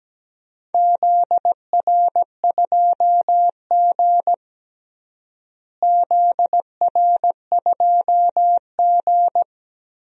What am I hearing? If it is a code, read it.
Morse code, heard as a keyed tone, about 17 wpm, 700 Hz. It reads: ZR2G ZR2G
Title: DXCC-100 Call Signs - 4 Characters - Repeated 3x - 17wpm